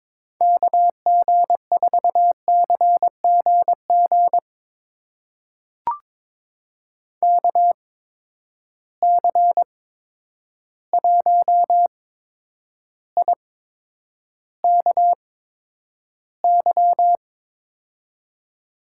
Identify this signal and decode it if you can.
Morse code, heard as a keyed tone, about 21 wpm, spaced as for 9 wpm, 700 Hz. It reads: KG4CGG KC1IKY
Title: USA Call Signs - 6 Characters - 8x Character Spacing - 22wpm